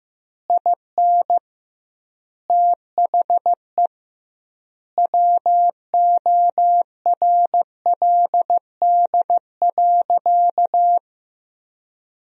Morse code, heard as a keyed tone, about 15 wpm, 700 Hz. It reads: IN THE WORLD.
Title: 2x Word-Spacing - Sentences from Top 200 Words - 15wpm